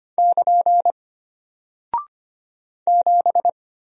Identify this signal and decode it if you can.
Morse code, heard as a keyed tone, about 25 wpm, 700 Hz. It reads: <KN> 7
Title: Single Character (Letter:Number:Punctuation:Prosign) Alt 25wpm